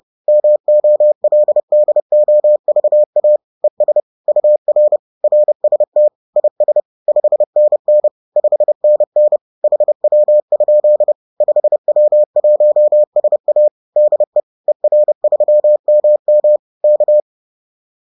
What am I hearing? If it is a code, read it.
Morse code, heard as a keyed tone, about 30 wpm, 600 Hz. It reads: MOLDOVA ES UR RST IS 5NN 5NN HW? 5W1SA DE ER3MM K